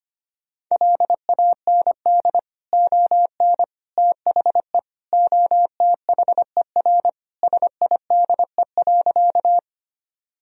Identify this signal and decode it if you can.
Morse code, heard as a keyed tone, about 25 wpm, 700 Hz. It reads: LAND ON THE OTHER SIDE.
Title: Sentences from Top 1000 Words - Review Part 1 - 25wpm